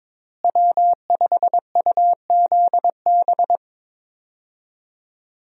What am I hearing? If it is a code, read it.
Morse code, heard as a keyed tone, about 22 wpm, 700 Hz. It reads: W5UZB
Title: USA Call Signs - 5 Characters - Repeated 3x - 22wpm